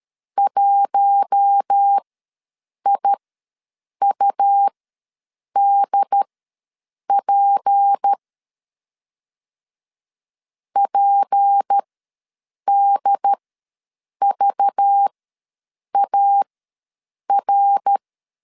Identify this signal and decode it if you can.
Morse code, heard as a keyed tone, about 13 wpm, spaced as for 7 wpm, 800 Hz. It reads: 1IUDP PDVAR
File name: LEZIONI / 6 PKX0 / 6 P ws 12wpm 3117